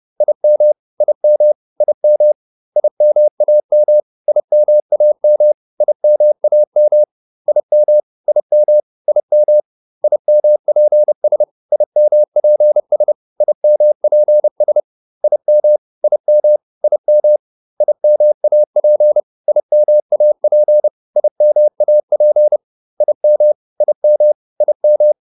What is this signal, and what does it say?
Morse code, heard as a keyed tone, about 30 wpm, 600 Hz. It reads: IM IM IM IMAM IMAM IMAM IM IM IM IMPS IMPS IMPS IM IM IM IMAP IMAP IMAP IM IM IM